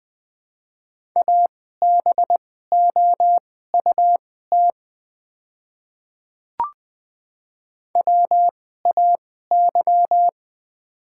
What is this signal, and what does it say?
Morse code, heard as a keyed tone, about 20 wpm, spaced as for 15 wpm, 690 Hz. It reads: ABOUT WAY
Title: 2x Character Spacing - Top 500 Words - 20wpm